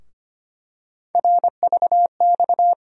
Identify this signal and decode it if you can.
Morse code, heard as a keyed tone, about 25 wpm, 700 Hz. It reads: RVX
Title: USA Call Sign Postfixes - 3 Letters 25wpm